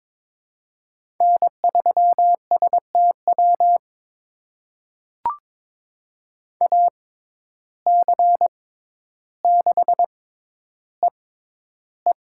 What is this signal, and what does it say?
Morse code, heard as a keyed tone, about 22 wpm, spaced as for 11 wpm, 700 Hz. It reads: N3STW AC6EE
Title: USA Call Signs - 5 Characters - 6x Character Spacing - 22wpm